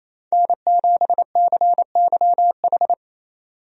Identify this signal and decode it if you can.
Morse code, heard as a keyed tone, about 28 wpm, 700 Hz. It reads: N7CYH
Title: USA Call Signs - 5 Characters - 28wpm